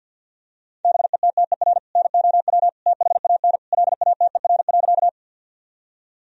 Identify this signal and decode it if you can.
Morse code, heard as a keyed tone, about 52 wpm, 700 Hz. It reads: BETTER NOW THAN LATER.
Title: Sentences from Top 1000 Words - Review Part 4 - 50wpm